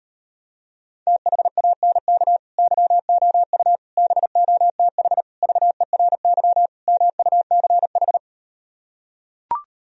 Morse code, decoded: THANK YOU BOTH VERY MUCH

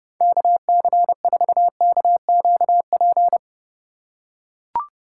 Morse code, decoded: KC4KQP

